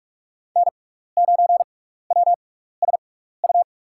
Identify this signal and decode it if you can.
Morse code, heard as a keyed tone, about 44 wpm, spaced as for 15 wpm, 700 Hz. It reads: N9WSU
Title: USA Call Signs - 6x Character Spacing - 45wpm